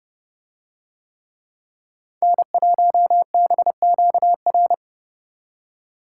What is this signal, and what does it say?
Morse code, heard as a keyed tone, about 29 wpm, 700 Hz. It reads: N1BQR